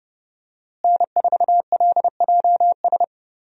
Morse code, heard as a keyed tone, about 30 wpm, 700 Hz. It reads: N4LJS
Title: USA Call Signs - 5 Characters - Repeated 3x - 30wpm